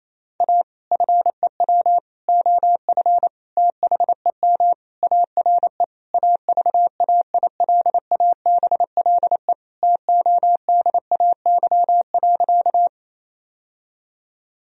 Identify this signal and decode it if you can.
Morse code, heard as a keyed tone, about 28 wpm, 690 Hz. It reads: A FEW OF THEM ARE AVAILABLE TODAY.
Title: Sentences from Top 1000 Words - Review Part 3 - 28wpm